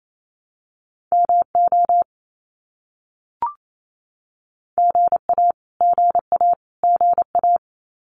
Morse code, heard as a keyed tone, about 28 wpm, 700 Hz. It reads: MO GA GA GA
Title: US States Repeated 3x 28wpm